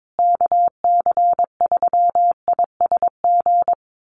Morse code, heard as a keyed tone, about 22 wpm, 700 Hz. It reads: KC3ISG
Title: USA Call Signs Repeated 3x 22wpm